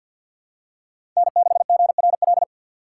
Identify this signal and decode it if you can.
Morse code, heard as a keyed tone, about 51 wpm, 700 Hz. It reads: N6GRL
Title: USA Call Signs - 5 Characters - 4x Character Spacing - 50wpm